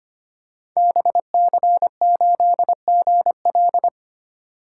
Morse code, heard as a keyed tone, about 25 wpm, 700 Hz. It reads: BC8GL
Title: Code Groups - 5 Letter:Numbers - 25wpm